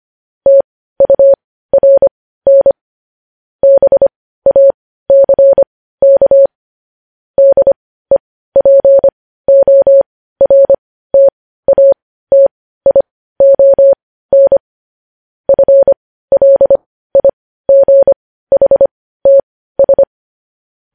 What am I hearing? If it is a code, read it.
Morse code, heard as a keyed tone, about 25 wpm, spaced as for 15 wpm, 550 Hz. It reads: TURN BACK DEPORTATION FLIGHTS
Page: news-headlines-in-morse-code-at-15-wpm